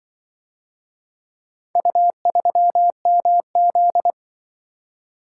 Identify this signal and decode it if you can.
Morse code, heard as a keyed tone, about 25 wpm, 700 Hz. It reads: U3MZ
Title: Speed-Racing - Four Letter-Number Combinations - 20wpm